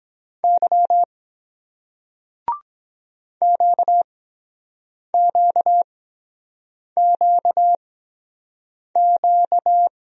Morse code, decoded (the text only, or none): Y QQQQ